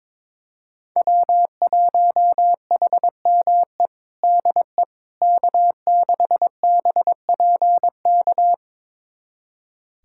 Morse code, heard as a keyed tone, about 22 wpm, 700 Hz. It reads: W1HME DE K6BPK
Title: USA Callsign DE Callsign - 5 Characters - 22wpm